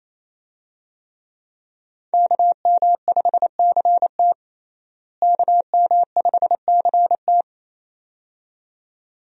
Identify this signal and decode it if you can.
Morse code, heard as a keyed tone, about 28 wpm, 700 Hz. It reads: KM5CT KM5CT
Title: USA Call Signs - 5 Characters - Repeated 2x - 28wpm